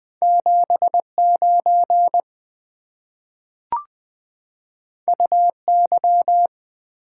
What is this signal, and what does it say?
Morse code, heard as a keyed tone, about 20 wpm, 690 Hz. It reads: 79 UY
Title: Two Letter-Number Combinations - 20wpm